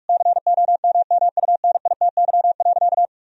Morse code, heard as a keyed tone, about 44 wpm, 700 Hz. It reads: COMMUNITY.
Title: Sentences from Top 1000 Words - Review Part 4 - 45wpm